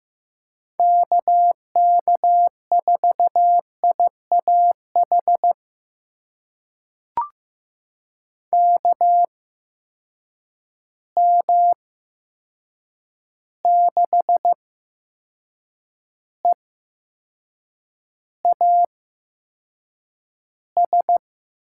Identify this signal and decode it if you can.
Morse code, heard as a keyed tone, about 15 wpm, spaced as for 6 wpm, 700 Hz. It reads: KK4IAH KM6EAS
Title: USA Call Signs - 6 Characters - 8x Character Spacing - 15wpm